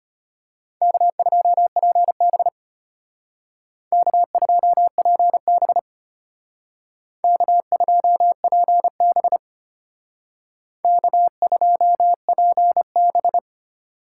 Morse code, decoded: K2PB K2PB K2PB K2PB

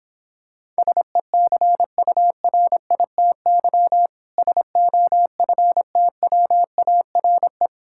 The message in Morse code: SECURITY SOFTWARE

